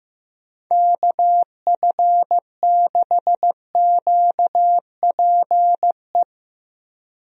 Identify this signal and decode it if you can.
Morse code, heard as a keyed tone, about 15 wpm, 700 Hz. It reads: KF6QPE
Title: USA Call Signs - 6 Characters - Repeated 3x - 15wpm